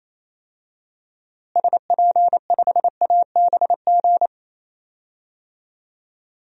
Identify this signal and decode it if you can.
Morse code, heard as a keyed tone, about 28 wpm, 700 Hz. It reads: SP5ABG
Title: DXCC-100 Call Signs - 6 Characters - Repeated 2x - 28wpm